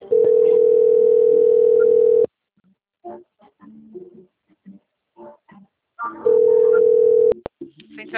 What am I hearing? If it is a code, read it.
Ringback tone (North America), 440 + 480 Hz, with speech.
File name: out- - - - - .2147